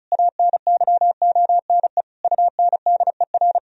Morse code, heard as a keyed tone, about 36 wpm, 700 Hz. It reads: ANYONE UNDER
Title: Sentences from Top 1000 Words - Review Part 1 - 35wpm